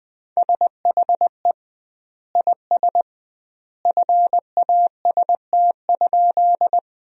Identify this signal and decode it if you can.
Morse code, heard as a keyed tone, about 21 wpm, 700 Hz. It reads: SHE IS FAST?
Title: 2x Word-Spacing - Sentences from Top 400 Words - 20wpm